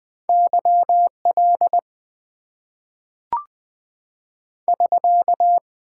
Morse code, heard as a keyed tone, about 20 wpm, 700 Hz. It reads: YL <SK>
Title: Top 90 QSO Elements - 8x Character Spacing - 20wpm